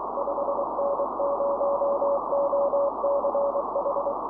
Morse code, heard as a keyed tone, about 23 wpm, 560 Hz, with noise. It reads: 5N0OCH
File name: boos / 5n0och-6m-cw